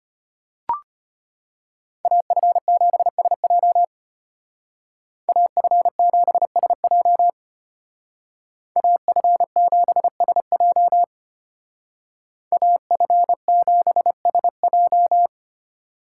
Morse code, heard as a keyed tone, about 31 wpm, 700 Hz. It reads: AF7SJ AF7SJ AF7SJ AF7SJ